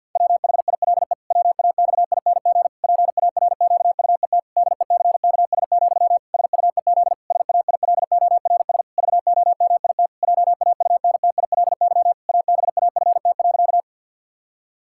Morse code, heard as a keyed tone, about 50 wpm, 700 Hz. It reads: WHILE WAXING PARQUET DECKS, SUEZ SAILORS VOMIT JAUNTILY ABAFT.
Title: Pangram Challenge v2 - 50wpm